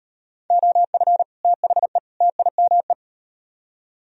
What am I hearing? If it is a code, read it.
Morse code, heard as a keyed tone, about 38 wpm, 700 Hz. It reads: OF THE TIME